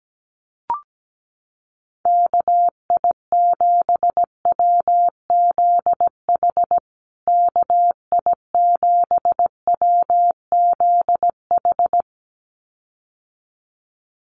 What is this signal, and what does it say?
Morse code, heard as a keyed tone, about 17 wpm, 700 Hz. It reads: KI7WZH KI7WZH
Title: USA Call Signs Repeated 2x 17wpm